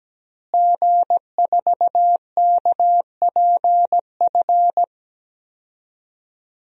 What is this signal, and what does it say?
Morse code, heard as a keyed tone, about 17 wpm, 700 Hz. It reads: G4KPF
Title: DXCC-100 Call Signs - 5 Characters - Repeated 3x - 17wpm